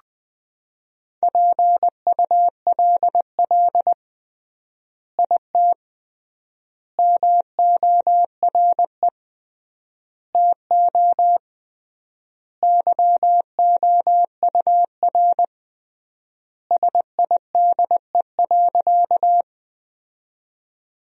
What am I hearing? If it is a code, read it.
Morse code, heard as a keyed tone, about 19 wpm, 700 Hz. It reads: PULL IT MORE TO YOUR SIDE.
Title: 3x Word-Spacing - Sentences from Top 400 Words - 20wpm